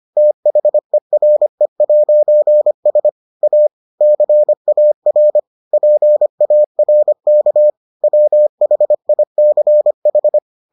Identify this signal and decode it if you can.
Morse code, heard as a keyed tone, about 25 wpm, 600 Hz. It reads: THERE'S A CAR PARK WHICH